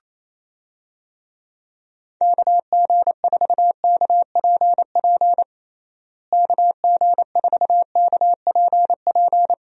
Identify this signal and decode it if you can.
Morse code, heard as a keyed tone, about 27 wpm, 700 Hz. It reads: KG4KPP KG4KPP
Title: USA Call Signs - 6 Characters - Repeated 3x - 28wpm